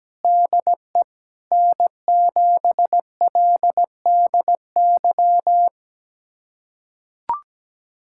Morse code, heard as a keyed tone, about 17 wpm, 700 Hz. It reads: DE N7LDY
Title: USA Callsign DE Callsign 17wpm